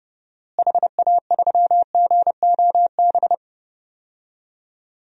Morse code, heard as a keyed tone, about 30 wpm, 700 Hz. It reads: HA3GOB
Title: DXCC-100 Call Signs - 6 Characters - Repeated 3x - 30wpm